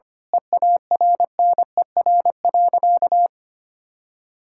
Morse code, heard as a keyed tone, about 25 wpm, 700 Hz. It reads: EARNER.
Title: Pangram Challenge v2 - 25wpm